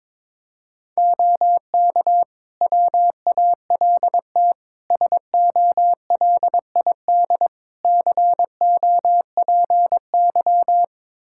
Morse code, heard as a keyed tone, about 22 wpm, 700 Hz. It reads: OK WALT SOLID COPY
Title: Practice QSO - Exchange 3 - More Info - 22wpm